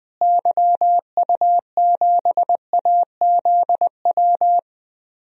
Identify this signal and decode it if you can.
Morse code, heard as a keyed tone, about 21 wpm, 700 Hz. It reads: YU7AZW
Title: DXCC-100 Call Signs - 6 Characters - Repeated 3x - 20wpm